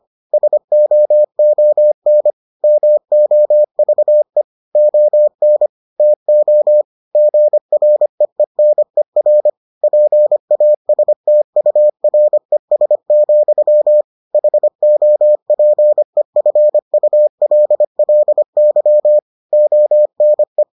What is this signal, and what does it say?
Morse code, heard as a keyed tone, about 25 wpm, 600 Hz. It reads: SOON MOVE ON TO GREENER PASTURES, HOPEFULLY ONE